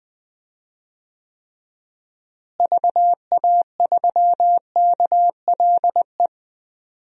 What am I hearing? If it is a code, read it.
Morse code, heard as a keyed tone, about 20 wpm, 700 Hz. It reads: VA3KLE
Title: DXCC-100 Call Signs - 6 Characters - Repeated 3x - 20wpm